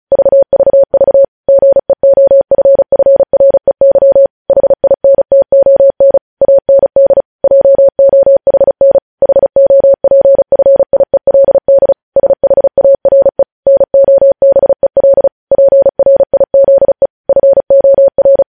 Morse code, heard as a keyed tone, about 36 wpm, 550 Hz. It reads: VVV GEOFFREY HINTON AND JOHN HOPFIELD SHARE NOBEL PRIZE FOR